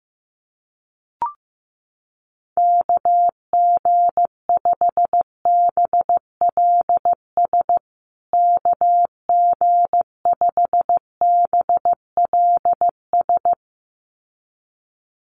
Morse code, heard as a keyed tone, about 15 wpm, 700 Hz. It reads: KG5BLS KG5BLS
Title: USA Call Signs Repeated 2x 15wpm